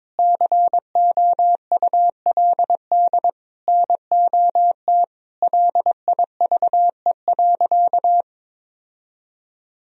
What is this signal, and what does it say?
Morse code, heard as a keyed tone, about 22 wpm, 700 Hz. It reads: COULD NOT LIVE.
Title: Sentences from Top 1000 Words - Review Part 4 - 22wpm